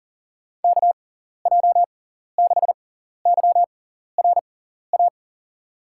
Morse code, heard as a keyed tone, about 40 wpm, spaced as for 14 wpm, 700 Hz. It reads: KJ6YRA
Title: USA Call Signs - 6 Characters - 6x Character Spacing - 40wpm